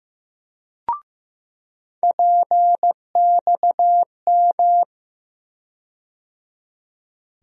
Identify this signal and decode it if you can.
Morse code, heard as a keyed tone, about 15 wpm, 700 Hz. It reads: PXM
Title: USA Call Sign Postfixes - 3 Letters 15wpm